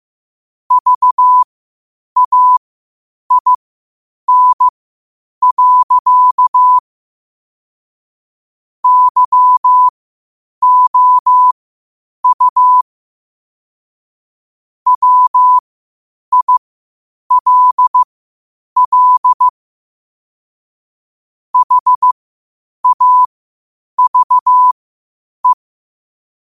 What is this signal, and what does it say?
Morse code, heard as a keyed tone, about 15 wpm, spaced as for 8 wpm, 1000 Hz. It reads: VAIN. YOU WILL HAVE
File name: qotdfiles / 8wpm